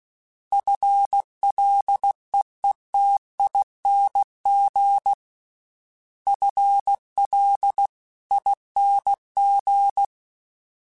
Morse code, heard as a keyed tone, about 16 wpm, 800 Hz. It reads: FLEETING FLING
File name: FL / FL mp3s / FL Prefixes 3x2 sp @15